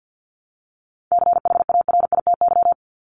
Morse code, heard as a keyed tone, about 50 wpm, 700 Hz. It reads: CHARITY
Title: Top 1501-1750 Words 50wpm